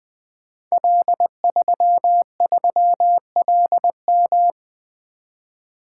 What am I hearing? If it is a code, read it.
Morse code, heard as a keyed tone, about 20 wpm, 700 Hz. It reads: L33LM
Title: Code Groups - 5 Letter:Numbers - 20wpm